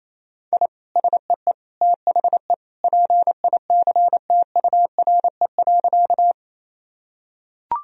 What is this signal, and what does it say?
Morse code, heard as a keyed tone, about 28 wpm, 700 Hz. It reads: I SEE THE PICTURE.
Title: Sentences from Top 1000 Words - Review Part 1 - 28wpm